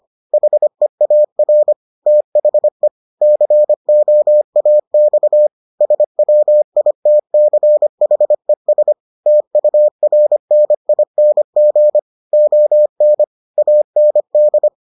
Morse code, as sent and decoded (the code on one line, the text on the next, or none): .... .- .-. / - .... / -.-. --- .- -..- / ... .-- .. - -.-. .... ... / - ..- .-. -. .. -. --. / --- -. / .- -. -..
HEAR THE COAX SWITCHES TURNING ON AND